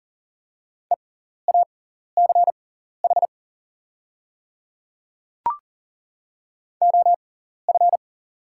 Morse code, heard as a keyed tone, about 40 wpm, spaced as for 14 wpm, 690 Hz. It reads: EACH OF
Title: 6x Character Spacing - Top 100 Words - 40wpm